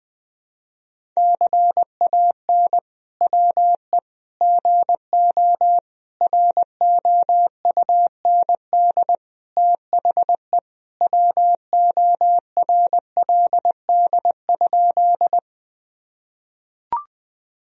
Morse code, heard as a keyed tone, about 20 wpm, 700 Hz. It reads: CAN WE GO ROUND THE WORLD?
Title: Sentences from Top 1000 Words - Review Part 1 - 20wpm